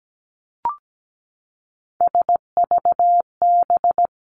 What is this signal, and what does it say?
Morse code, heard as a keyed tone, about 17 wpm, 700 Hz. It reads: SVB
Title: Three Challenging Letter-Number Combinations 17wpm